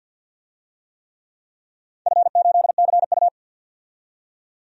Morse code, heard as a keyed tone, about 51 wpm, 700 Hz. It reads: F8CU
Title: DXCC-100 Call Signs - 4 Characters - 50wpm